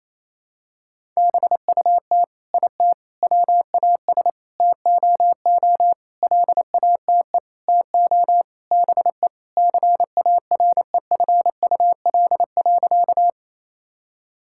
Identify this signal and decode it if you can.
Morse code, heard as a keyed tone, about 28 wpm, 700 Hz. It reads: BUT IT WAS TOO LATE TO BE CAREFUL.
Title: Sentences from Top 1600 Words 28wpm